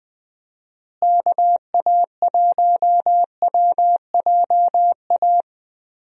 Morse code, heard as a keyed tone, about 21 wpm, 700 Hz. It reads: KA1WJA
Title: USA Call Signs - 6 Characters - Repeated 3x - 20wpm